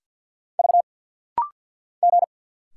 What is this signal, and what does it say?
Morse code, heard as a keyed tone, about 50 wpm, 700 Hz. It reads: V G
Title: Single Letter-Number Alt - Rapid-Fire 50wpm